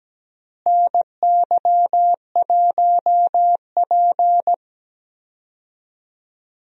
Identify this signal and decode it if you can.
Morse code, heard as a keyed tone, about 17 wpm, 700 Hz. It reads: NY1P
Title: USA Call Signs - 4 Characters - Repeated 2x - 17wpm